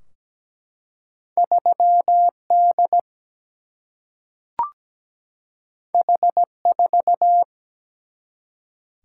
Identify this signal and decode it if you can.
Morse code, heard as a keyed tone, about 17 wpm, 700 Hz. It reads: 3D H4
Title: Single-Letter Single-Number 17wpm